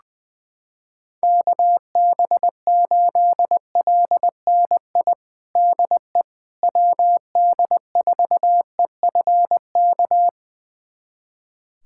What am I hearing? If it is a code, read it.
Morse code, heard as a keyed tone, about 20 wpm, 700 Hz. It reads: KB8LNI DE WD4EFK
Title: USA Callsign DE Callsign 20wpm